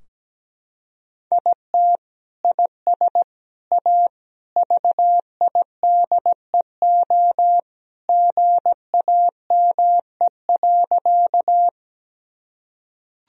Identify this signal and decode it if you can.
Morse code, heard as a keyed tone, about 17 wpm, 700 Hz. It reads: IT IS A VIDEO GAME.
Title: Sentences from Top 1000 Words - Review Part 4 - 17wpm